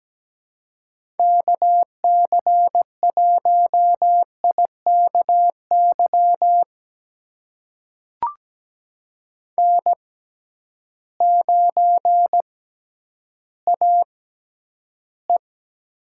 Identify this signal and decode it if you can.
Morse code, heard as a keyed tone, about 17 wpm, spaced as for 10 wpm, 700 Hz. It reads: KC1IKY N9AE
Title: USA Call Signs - 6x Character Spacing - 17wpm